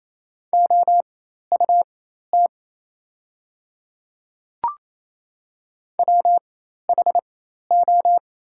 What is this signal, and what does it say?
Morse code, heard as a keyed tone, about 28 wpm, spaced as for 13 wpm, 690 Hz. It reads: OUT WHO
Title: 4x Character Spacing - Top 200 Words - 28wpm